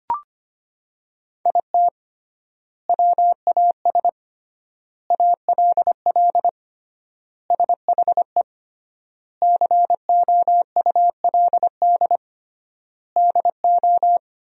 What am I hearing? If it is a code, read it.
Morse code, heard as a keyed tone, about 25 wpm, 700 Hz. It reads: IT WAS ALL SHE COULD DO
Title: 3x Word-Spacing - Sentences from Top 400 Words - 25wpm